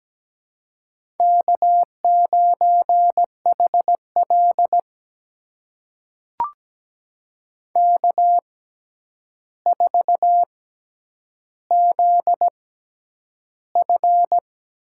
Morse code, heard as a keyed tone, about 17 wpm, spaced as for 9 wpm, 700 Hz. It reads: K9HL K4ZF
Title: USA Call Signs - 4 Characters - 6x Character Spacing - 17wpm